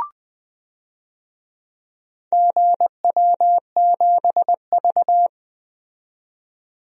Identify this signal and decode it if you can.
Morse code, heard as a keyed tone, about 20 wpm, 700 Hz. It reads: GW7V
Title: DXCC-100 Call Signs - 4 Characters - 20wpm